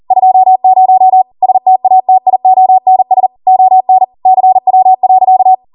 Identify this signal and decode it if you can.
Morse code, heard as a keyed tone, about 40 wpm, 750 Hz. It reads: 20 STATIONS ON CW.